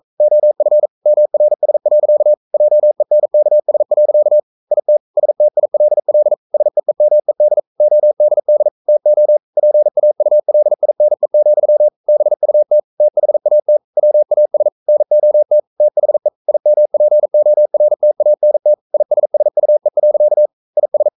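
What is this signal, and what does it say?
Morse code, heard as a keyed tone, about 42 wpm, 600 Hz. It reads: OF MRS. JENKS. IT STILL SEEMED ODD TO PAULINE, BUT THAT WAS NOT THE IMPORTANT ISSUE. IS
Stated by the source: Morse code sending of an English novel at 42 wpm.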